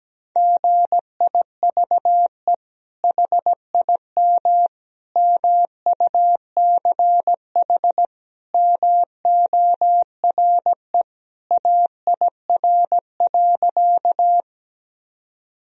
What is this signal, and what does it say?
Morse code, heard as a keyed tone, about 17 wpm, 700 Hz. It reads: GIVE HIM MUCH MORE AIR.
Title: Sentences from Top 1000 Words - Review Part 1 - 17wpm